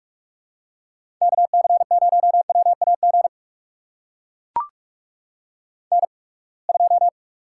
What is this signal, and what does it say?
Morse code, heard as a keyed tone, about 44 wpm, spaced as for 30 wpm, 700 Hz. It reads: KC0WAG N2
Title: USA Call Signs - 8x Character Spacing - 45wpm